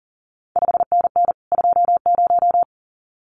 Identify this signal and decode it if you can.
Morse code, heard as a keyed tone, about 40 wpm, 700 Hz. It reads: 5NN 20
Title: CQ WW DX Contest Exchange Encore - 40wpm